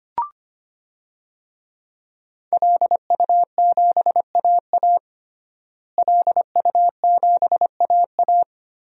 Morse code, heard as a keyed tone, about 25 wpm, 700 Hz. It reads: LU7AA LU7AA
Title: DXCC-100 Call Signs - 5 Characters - Repeated 3x - 25wpm